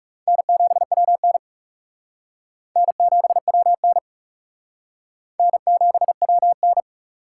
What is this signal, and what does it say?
Morse code, heard as a keyed tone, about 38 wpm, 700 Hz. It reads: N7WN N7WN N7WN